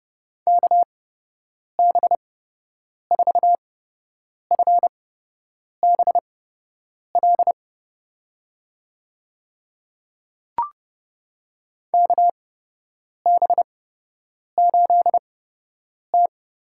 Morse code, heard as a keyed tone, about 32 wpm, spaced as for 8 wpm, 700 Hz. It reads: KB4FBL KB8T
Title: USA Call Signs - 6 Characters - 8x Character Spacing - 30wpm